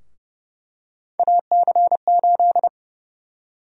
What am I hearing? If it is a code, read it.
Morse code, heard as a keyed tone, about 32 wpm, 700 Hz. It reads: AC8
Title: USA Call Sign Prefixes - 30wpm